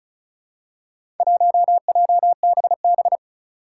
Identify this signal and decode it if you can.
Morse code, heard as a keyed tone, about 36 wpm, 700 Hz. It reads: 1JBB
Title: Four Challenging Letter-Number Combinations 35wpm